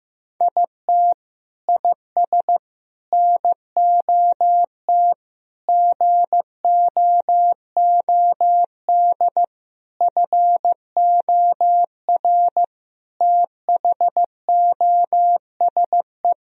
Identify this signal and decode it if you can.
Morse code, heard as a keyed tone, about 15 wpm, 700 Hz. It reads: IT IS NOT GOOD FOR THOSE
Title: Sentences from Top 1000 Words - Review Part 2 - 15wpm